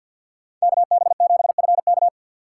Morse code, heard as a keyed tone, about 52 wpm, 700 Hz. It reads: KB7FK
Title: USA Call Signs - 5 Characters - 3x Character Spacing - 50wpm